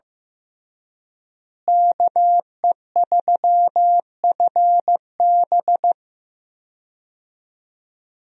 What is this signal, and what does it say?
Morse code, heard as a keyed tone, about 15 wpm, 700 Hz. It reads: KE3FB
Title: USA Call Signs - 5 Characters - Repeated 2x - 15wpm